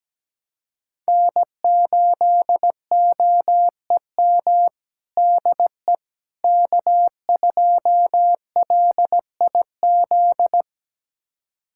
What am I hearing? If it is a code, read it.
Morse code, heard as a keyed tone, about 17 wpm, 700 Hz. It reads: N8OEM DE K2LIZ
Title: USA Callsign DE Callsign 17wpm